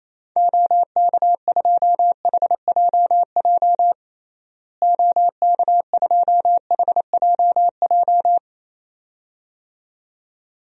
Morse code, heard as a keyed tone, about 28 wpm, 700 Hz. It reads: OK2HJJ OK2HJJ
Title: DXCC-100 Call Signs - 6 Characters - Repeated 2x - 28wpm